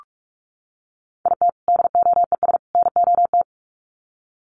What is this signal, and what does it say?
Morse code, heard as a keyed tone, about 44 wpm, 700 Hz. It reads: IT DOES NOT
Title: Sets of 3 Words 45wpm